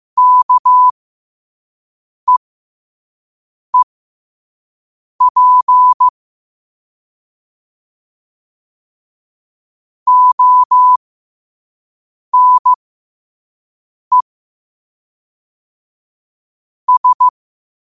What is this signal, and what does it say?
Morse code, heard as a keyed tone, about 15 wpm, spaced as for 5 wpm, 1000 Hz. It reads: KEEP ONE S
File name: qotdfiles / 5wpm